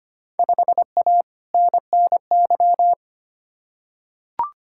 Morse code, heard as a keyed tone, about 25 wpm, 700 Hz. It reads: 5A NNY
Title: Field Day Exchanges - 25wpm